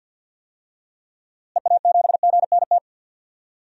Morse code, heard as a keyed tone, about 50 wpm, 700 Hz. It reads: EA7GNT
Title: DXCC-100 Call Signs - 6 Characters - Repeated 3x - 50wpm